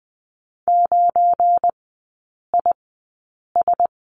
Morse code, heard as a keyed tone, about 20 wpm, spaced as for 8 wpm, 690 Hz. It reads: 9IS
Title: 198 Lesson 32e - Letter k - Three Characters 20wpm